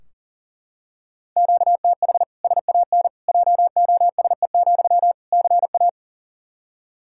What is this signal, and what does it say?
Morse code, heard as a keyed tone, about 40 wpm, 700 Hz. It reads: QTH SAN JOSE, CA
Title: Practice QSO - Exchange 1 - Beginning - 40wpm